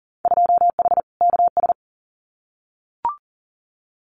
Morse code, heard as a keyed tone, about 41 wpm, 700 Hz. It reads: 2H KS
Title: Winter Field Day Exchanges Encore 40wpm